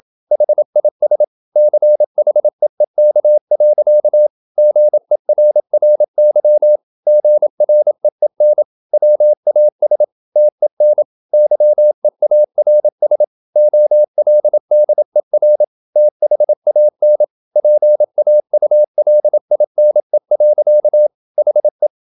Morse code, heard as a keyed tone, about 27 wpm, 600 Hz. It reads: HIS CHEEK. GERRY GREEN WAS TEN YEARS OLDER THAN PAULINE. HE
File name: Never 0027wpm